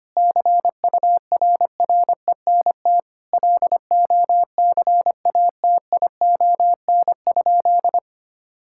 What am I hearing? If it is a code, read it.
Morse code, heard as a keyed tone, about 25 wpm, 700 Hz. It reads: CURRENT LOCATION?
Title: Sentences from Top 1000 Words - Review Part 6 - 25wpm